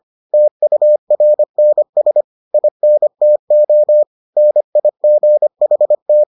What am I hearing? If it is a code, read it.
Morse code, heard as a keyed tone, about 25 wpm, 600 Hz. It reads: TURNS INTO NIGHT